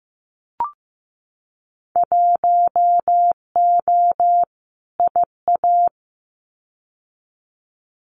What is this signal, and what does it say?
Morse code, heard as a keyed tone, about 15 wpm, 700 Hz. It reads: 1O IA
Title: Winter Field Day Exchanges 15wpm